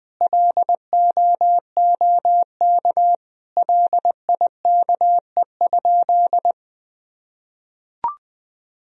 Morse code, heard as a keyed tone, about 21 wpm, 700 Hz. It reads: LOOK LIKE?
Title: Sentences from Top 100 Words and Abbreviations 20wpm